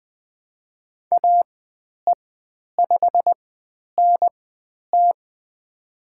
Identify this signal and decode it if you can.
Morse code, heard as a keyed tone, about 20 wpm, spaced as for 10 wpm, 700 Hz. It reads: AE5NT
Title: 095 Lesson 16g - Letter c - Call Signs 20wpm